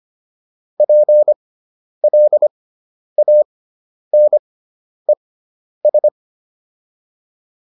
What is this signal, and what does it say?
Morse code, heard as a keyed tone, about 25 wpm, spaced as for 10 wpm, 600 Hz. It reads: PLANES